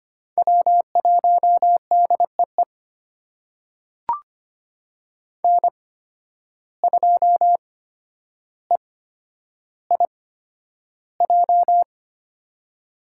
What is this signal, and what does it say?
Morse code, heard as a keyed tone, about 25 wpm, spaced as for 11 wpm, 700 Hz. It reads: W1DEE N2EIJ
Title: USA Call Signs - 5 Characters - 8x Character Spacing - 25wpm